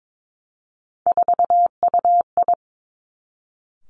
Morse code, heard as a keyed tone, about 22 wpm, 700 Hz. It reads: 4UI
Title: Three Challenging Letter-Number Combinations 22wpm